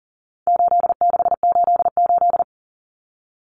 Morse code, heard as a keyed tone, about 37 wpm, 700 Hz. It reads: 8688
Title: Numbers - 4-Digit 40wpm